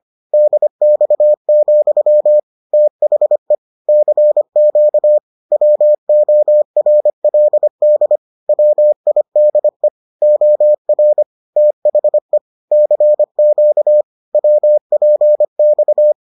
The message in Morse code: DX, THE CQ WORLD WIDE OR THE CQ WPX

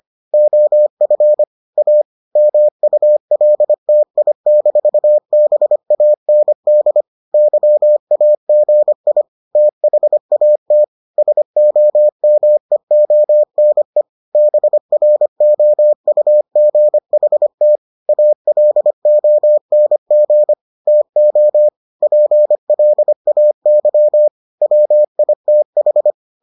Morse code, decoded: OF A MULTI-BAND YAGI THAT SOMEONE BROUGHT ALONG TO PLAY WITH

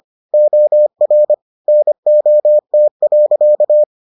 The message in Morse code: OR NOT.

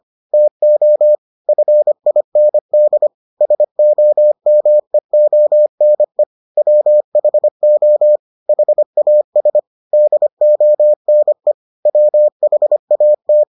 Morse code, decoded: TO FIND SOMEONE WHO HAS DONE WHAT

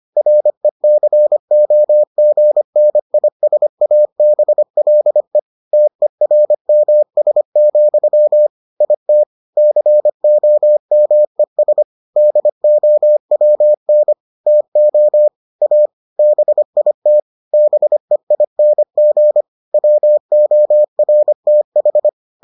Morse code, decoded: RECOGNISABLE TERMS, IT COMES DOWN TO A BIT BEING WORTH

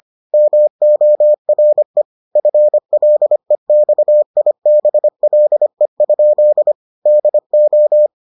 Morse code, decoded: MORE FLEXIBLE? DO

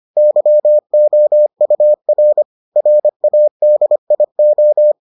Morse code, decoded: YOUR RADIO

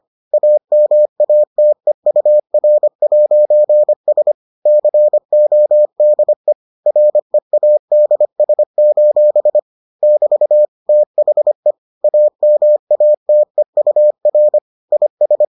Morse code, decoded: AMATEUR'S CODE READS: <BT> THE AMATEUR IS

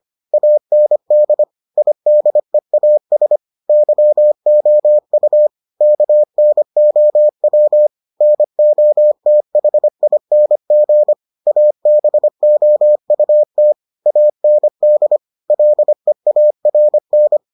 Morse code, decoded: AND IDEAS YOU KNOW NOTHING ABOUT AND LEARN

